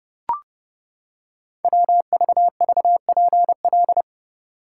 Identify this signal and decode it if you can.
Morse code, heard as a keyed tone, about 28 wpm, 700 Hz. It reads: WVVPL
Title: Code Groups - 5 Letter:Numbers - 30wpm